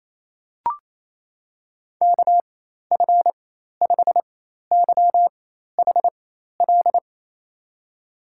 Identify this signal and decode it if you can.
Morse code, heard as a keyed tone, about 27 wpm, spaced as for 13 wpm, 700 Hz. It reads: KF5YHL
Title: USA Call Signs - 6 Characters - 4x Character Spacing - 28wpm